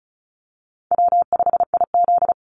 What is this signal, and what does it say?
Morse code, heard as a keyed tone, about 34 wpm, 700 Hz. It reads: W5IZ
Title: Easy Call Signs Repeated 3x - 35wpm